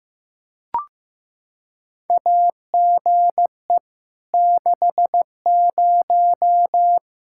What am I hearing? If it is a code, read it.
Morse code, heard as a keyed tone, about 15 wpm, 700 Hz. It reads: AGE 60
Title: Practice QSO - Exchange 3 - More Info - 15wpm